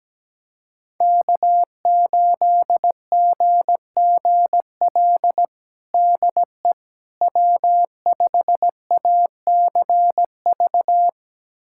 Morse code, heard as a keyed tone, about 17 wpm, 700 Hz. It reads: K8GGL DE W5ACV
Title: USA Callsign DE Callsign - 5 Characters - 17wpm